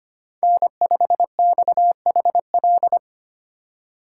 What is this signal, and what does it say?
Morse code, heard as a keyed tone, about 25 wpm, 700 Hz. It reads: N5XHL
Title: USA Call Signs - 5 Characters - 3x Character Spacing - 25wpm